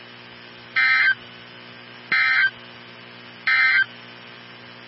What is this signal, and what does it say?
Emergency Alert System (SAME) tones, heard as harsh buzzing bursts. It is heard from 0.6 to 3.9 s.
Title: NOAA Weather Radio 162.550